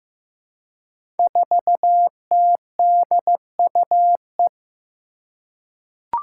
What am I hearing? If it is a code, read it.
Morse code, heard as a keyed tone, about 15 wpm, 700 Hz. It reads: 4TDUE
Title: Five Challenging Letter-Number Combinations 15wpm